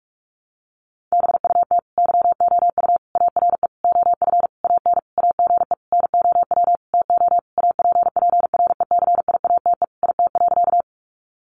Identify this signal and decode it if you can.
Morse code, heard as a keyed tone, about 44 wpm, 700 Hz. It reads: BUT YOU ARE OF AN AGE NOW TO APPRECIATE IT.
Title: Sentences from Top 1300 Words 45wpm